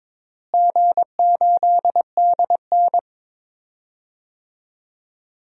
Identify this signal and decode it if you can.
Morse code, heard as a keyed tone, about 22 wpm, 700 Hz. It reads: G8DN
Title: DXCC-100 Call Signs - 4 Characters - Repeated 3x - 22wpm